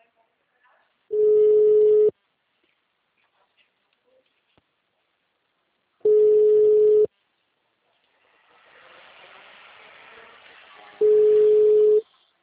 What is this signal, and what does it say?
Ringback tone (Europe, 425 Hz), with speech.
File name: out- -2040- - - .178